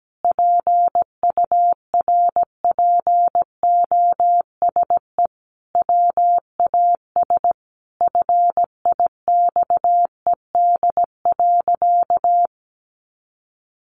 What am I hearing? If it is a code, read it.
Morse code, heard as a keyed tone, about 17 wpm, 700 Hz. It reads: PURPOSE WAS FIXED.
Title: Sentences from Top 1100 Words 17wpm